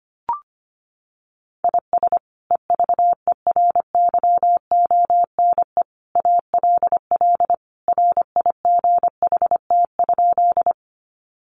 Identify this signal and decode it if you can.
Morse code, heard as a keyed tone, about 25 wpm, 700 Hz. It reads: IS EVERYONE ALL RIGHT?
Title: Sentences from Top 800 Words 25wpm